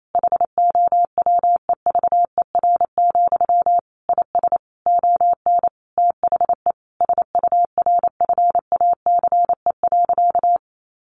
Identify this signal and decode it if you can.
Morse code, heard as a keyed tone, about 28 wpm, 700 Hz. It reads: HOWEVER, IS ON THE SURFACE.
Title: Sentences from Top 600 Words 28wpm